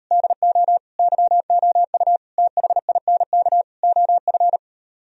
Morse code, heard as a keyed tone, about 38 wpm, 700 Hz. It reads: DO YOU THINK OF